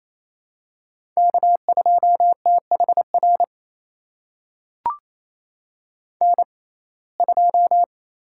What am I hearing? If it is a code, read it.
Morse code, heard as a keyed tone, about 28 wpm, spaced as for 20 wpm, 700 Hz. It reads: K2THR N2
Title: USA Call Signs - 5 Characters - 6x Character Spacing - 28wpm